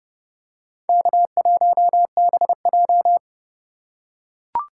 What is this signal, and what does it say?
Morse code, heard as a keyed tone, about 30 wpm, 700 Hz. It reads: K1BJ
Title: USA Call Signs - 4 Characters - 3x Character Spacing - 30wpm